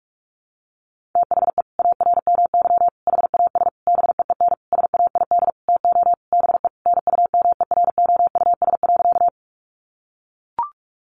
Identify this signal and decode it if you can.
Morse code, heard as a keyed tone, about 44 wpm, 700 Hz. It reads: THE ARMY HAS BEEN SAID TO BE NUMEROUS.
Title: Sentences from Top 1400 Words 45wpm